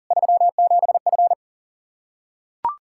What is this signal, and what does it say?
Morse code, heard as a keyed tone, about 40 wpm, 700 Hz. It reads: 37F
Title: Practice QSO - Exchange 2 - Info - 40wpm